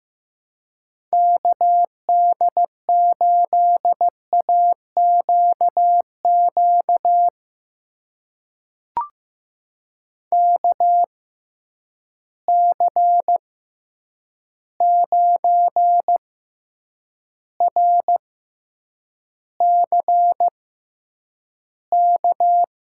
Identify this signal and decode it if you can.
Morse code, heard as a keyed tone, about 15 wpm, spaced as for 8 wpm, 700 Hz. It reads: KD8AQQ KC9RCK
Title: USA Call Signs - 6 Characters - 6x Character Spacing - 15wpm